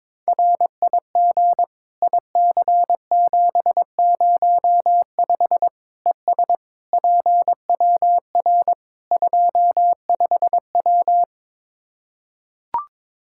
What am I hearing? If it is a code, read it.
Morse code, heard as a keyed tone, about 22 wpm, 700 Hz. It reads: RIG IC705 ES PWR 25W
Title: Practice QSO - Exchange 2 - Info - 22wpm